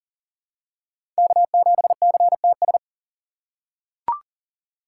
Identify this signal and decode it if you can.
Morse code, heard as a keyed tone, about 37 wpm, 690 Hz. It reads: K7CTS
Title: USA Call Signs - 5 Characters - 3x Character Spacing - 40wpm